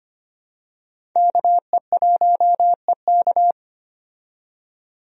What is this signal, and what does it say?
Morse code, heard as a keyed tone, about 25 wpm, 700 Hz. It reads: KE1EK
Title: USA Call Signs - 5 Characters - 25wpm